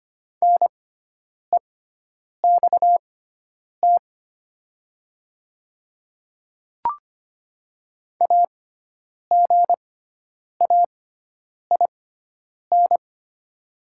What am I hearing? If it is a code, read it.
Morse code, heard as a keyed tone, about 25 wpm, spaced as for 9 wpm, 700 Hz. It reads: NEXT AGAIN
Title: 6x Character Spacing - Top 500 Words - 25wpm